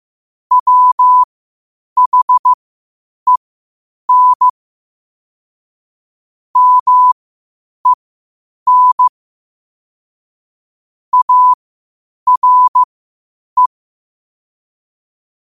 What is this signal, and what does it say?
Morse code, heard as a keyed tone, about 15 wpm, spaced as for 8 wpm, 1000 Hz. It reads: WHEN MEN ARE
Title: 8 WPM morse code quotes for Thu, 29 Feb 2024